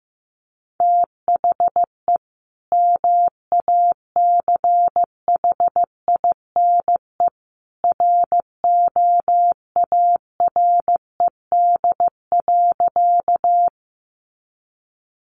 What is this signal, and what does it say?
Morse code, heard as a keyed tone, about 15 wpm, 700 Hz. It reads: THE MACHINE ROARED.
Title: A Sound Of Thunder 15wpm